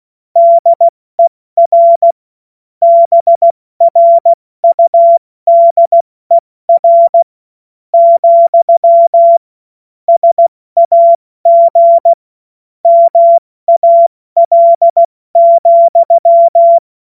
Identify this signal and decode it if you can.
Morse code, heard as a keyed tone, about 16 wpm, 680 Hz. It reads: DER BRUDER , SAG MAL,
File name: Witz-05-August-2025-70-F80